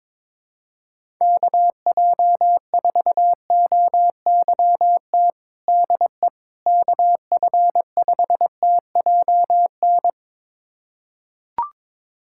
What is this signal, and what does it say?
Morse code, heard as a keyed tone, about 21 wpm, 700 Hz. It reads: KJ4OYT DE KF5TJN
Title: USA Callsign DE Callsign - 6 Characters - 22wpm